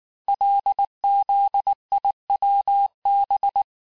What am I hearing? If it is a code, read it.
Morse code, heard as a keyed tone, about 19 wpm, 800 Hz. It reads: LZIWB
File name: cours-du-lundi files / cours-1595-teste-CHIFFRE-vitesse18-MPM